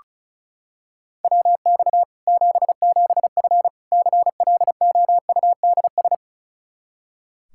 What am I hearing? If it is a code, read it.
Morse code, heard as a keyed tone, about 36 wpm, 700 Hz. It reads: WX 77F CLOUDS
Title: Practice QSO - Exchange 2 - Info - 35wpm